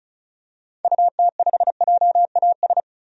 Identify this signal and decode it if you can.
Morse code, heard as a keyed tone, about 34 wpm, 700 Hz. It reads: UT5JAS
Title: DXCC-100 Call Signs - 6 Characters - 35wpm